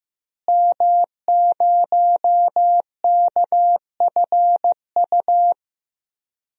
Morse code, heard as a keyed tone, about 15 wpm, 700 Hz. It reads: M0KFU
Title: DXCC-100 Call Signs - 5 Characters - Repeated 2x - 15wpm